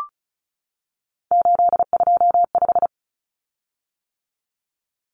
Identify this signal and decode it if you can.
Morse code, heard as a keyed tone, about 34 wpm, 700 Hz. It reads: 825
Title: Common Cut Numbers - 3-Digit 35wpm